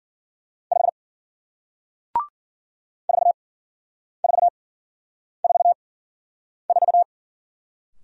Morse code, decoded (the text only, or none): 5 4444